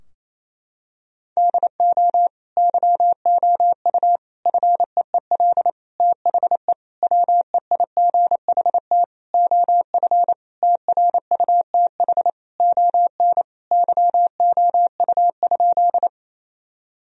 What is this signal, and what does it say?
Morse code, heard as a keyed tone, about 28 wpm, 700 Hz. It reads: DO YOU FEEL THE WEIGHT OF TRUTH ON YOU?
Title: Sentences from Top 1000 Words - Review Part 3 - 28wpm